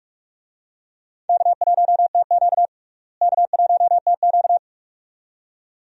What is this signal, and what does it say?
Morse code, heard as a keyed tone, about 43 wpm, 700 Hz. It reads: K1TQ K1TQ
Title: USA Call Signs - 4 Characters - Repeated 2x - 45wpm